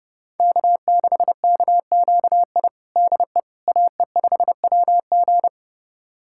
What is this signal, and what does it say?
Morse code, heard as a keyed tone, about 30 wpm, 700 Hz. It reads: K6KQI DE AE5WG
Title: USA Callsign DE Callsign - 5 Characters - 30wpm